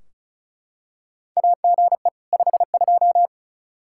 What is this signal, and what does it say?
Morse code, heard as a keyed tone, about 36 wpm, 700 Hz. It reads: AGE 52
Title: Practice QSO - Exchange 3 - More Info - 35wpm